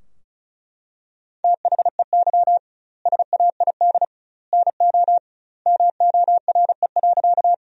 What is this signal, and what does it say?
Morse code, heard as a keyed tone, about 36 wpm, 700 Hz. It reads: THEY SAID NO MORE.
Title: 2x Word-Spacing - Sentences from Top 100 Words - 35wpm